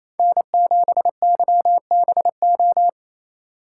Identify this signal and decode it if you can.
Morse code, heard as a keyed tone, about 27 wpm, 700 Hz. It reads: N7YBO
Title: USA Call Signs - 5 Characters - 4x Character Spacing - 28wpm